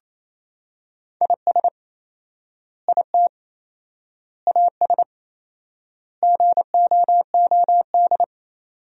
Morse code, heard as a keyed tone, about 28 wpm, 700 Hz. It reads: IS IT AS GOOD
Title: 4x Word-Spacing - Sentences from Top 200 Words - 28wpm